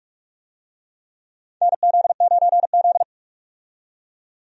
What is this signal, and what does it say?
Morse code, heard as a keyed tone, about 45 wpm, 700 Hz. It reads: NZ9Z
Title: USA Call Signs - 4 Characters - 45wpm